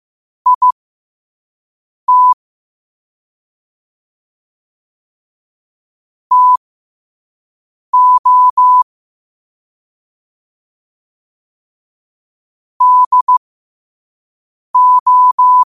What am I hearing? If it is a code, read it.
Morse code, heard as a keyed tone, about 15 wpm, spaced as for 5 wpm, 1000 Hz. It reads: IT TO DO
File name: qotdfiles / 5wpm